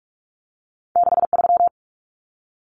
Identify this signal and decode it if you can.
Morse code, heard as a keyed tone, about 46 wpm, 700 Hz. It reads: 63
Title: Common Cut Numbers - 2-Digit 45wpm